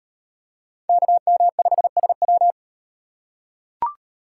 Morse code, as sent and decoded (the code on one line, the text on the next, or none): -.- -- ..... ... .--
KM5SW